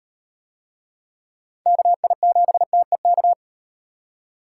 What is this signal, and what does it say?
Morse code, heard as a keyed tone, about 34 wpm, 700 Hz. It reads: KI7TEK